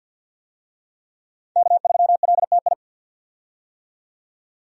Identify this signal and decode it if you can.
Morse code, heard as a keyed tone, about 45 wpm, 700 Hz. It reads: K3LTI
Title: USA Call Signs - 5 Characters - 50wpm